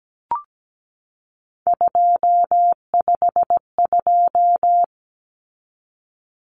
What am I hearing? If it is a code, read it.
Morse code, heard as a keyed tone, about 17 wpm, 700 Hz. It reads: 252
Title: Common Cut Numbers - 3-Digit 17wpm